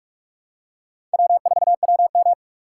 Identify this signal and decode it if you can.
Morse code, heard as a keyed tone, about 39 wpm, 700 Hz. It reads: W4WM